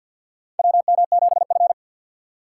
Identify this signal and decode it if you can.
Morse code, heard as a keyed tone, about 48 wpm, 700 Hz. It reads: WM7F
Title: USA Call Signs - 4x Character Spacing - 50wpm